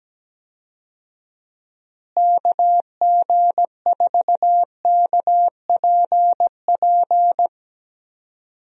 Morse code, heard as a keyed tone, about 17 wpm, 700 Hz. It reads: KG4KPP
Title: USA Call Signs - 6 Characters - Repeated 3x - 17wpm